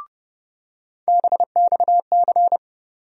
Morse code, heard as a keyed tone, about 30 wpm, 700 Hz. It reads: BXC
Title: USA Call Sign Postfixes - 3 Letters 30wpm